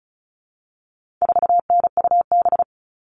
Morse code, heard as a keyed tone, about 34 wpm, 700 Hz. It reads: 4NUB
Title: All Cut Numbers - 4-Digit 35wpm